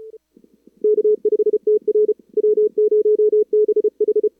Morse code, heard as a keyed tone, about 37 wpm, 440 Hz. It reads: K5TR W0BH
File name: CWSprint / Feb2022 / k5tr-feb-na-sprint-cw-k5ot -0000